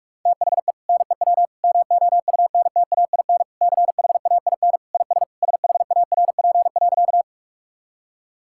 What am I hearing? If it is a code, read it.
Morse code, heard as a keyed tone, about 44 wpm, 700 Hz. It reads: THE NEW MOUNTAIN CHAIN IS SHARP.
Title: Sentences from Top 1600 Words 45wpm